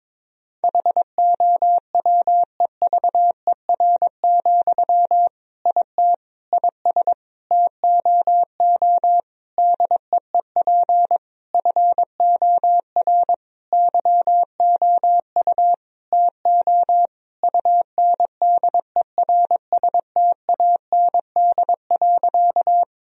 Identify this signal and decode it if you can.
Morse code, heard as a keyed tone, about 22 wpm, 700 Hz. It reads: HOWEVER, IT IS TOO DEEP FOR YOU TO UNDERSTAND.
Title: Sentences from Top 1000 Words - Review Part 3 - 22wpm